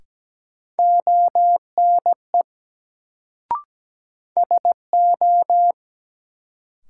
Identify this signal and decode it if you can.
Morse code, heard as a keyed tone, about 17 wpm, 700 Hz. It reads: ONE SO
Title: Sets of 2 Words - New Word Drill - 17wpm